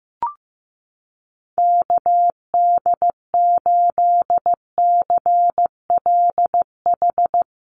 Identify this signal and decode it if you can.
Morse code, heard as a keyed tone, about 15 wpm, 700 Hz. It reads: KD8CLH
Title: USA Call Signs Repeated 2x 15wpm